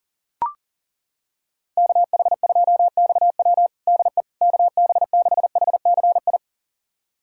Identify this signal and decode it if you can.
Morse code, heard as a keyed tone, about 40 wpm, 700 Hz. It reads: KH2XW DE KB6HCI
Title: USA Callsign DE Callsign 40wpm